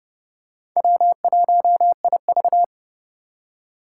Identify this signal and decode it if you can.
Morse code, heard as a keyed tone, about 30 wpm, 700 Hz. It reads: W1IV
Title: Four Challenging Letter-Number Combinations 30wpm